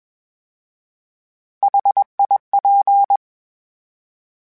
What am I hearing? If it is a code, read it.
Morse code, heard as a keyed tone, about 21 wpm, 800 Hz. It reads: HIP